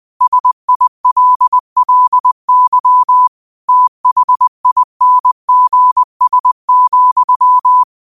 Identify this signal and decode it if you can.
Morse code, heard as a keyed tone, about 20 wpm, 1000 Hz. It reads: SILLY THINGS,
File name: qotdfiles / 20wpm